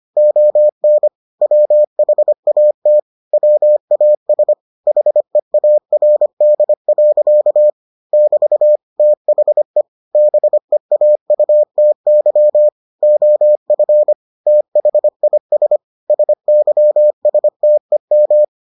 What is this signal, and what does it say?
Morse code, heard as a keyed tone, about 25 wpm, 600 Hz. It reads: ON WHAT WAS HEARD. <BT> THE BEAUTY OF THIS SYSTEM